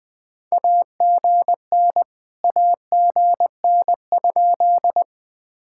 Morse code, heard as a keyed tone, about 22 wpm, 700 Hz. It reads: AGN AGN?
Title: POTA QSO Practice - 20wpm